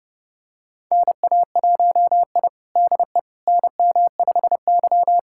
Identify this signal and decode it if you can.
Morse code, heard as a keyed tone, about 32 wpm, 700 Hz. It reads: NA1I DE NM5Y
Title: USA Callsign DE Callsign - 4 Characters - 30wpm